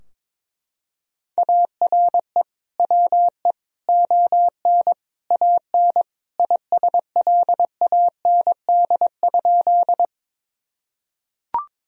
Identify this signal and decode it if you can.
Morse code, heard as a keyed tone, about 22 wpm, 700 Hz. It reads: ARE WE ON AN ISLAND?
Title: Sentences from Top 1000 Words - Review Part 3 - 22wpm